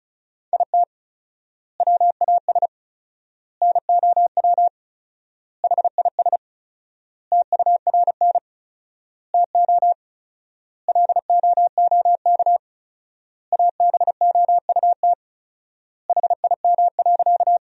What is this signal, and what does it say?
Morse code, heard as a keyed tone, about 36 wpm, 700 Hz. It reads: IT WAS NOW HIS TURN TO LOOK ABOUT HIM.
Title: 4x Word-Spacing - Sentences from Top 200 Words - 35wpm